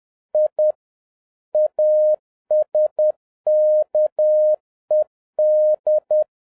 Morse code, heard as a keyed tone, about 10 wpm, 610 Hz. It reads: I ASKED